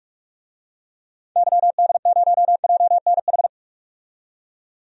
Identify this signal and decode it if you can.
Morse code, heard as a keyed tone, about 46 wpm, 700 Hz. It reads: YD0JNH
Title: DXCC-100 Call Signs - 6 Characters - 45wpm